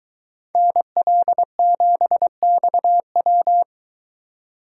Morse code, heard as a keyed tone, about 21 wpm, 700 Hz. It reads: NL7XW